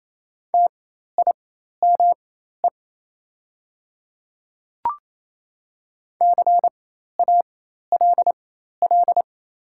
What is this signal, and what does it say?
Morse code, heard as a keyed tone, about 27 wpm, spaced as for 13 wpm, 700 Hz. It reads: TIME CALL
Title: 4x Character Spacing - Top 200 Words - 28wpm